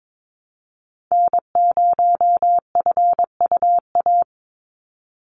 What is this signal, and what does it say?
Morse code, heard as a keyed tone, about 22 wpm, 700 Hz. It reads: N0FUA
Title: USA Call Signs 22wpm